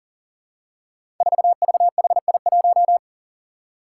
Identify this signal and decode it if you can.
Morse code, heard as a keyed tone, about 40 wpm, 700 Hz. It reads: 4VHI1
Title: Five Challenging Letter-Number Combinations 40wpm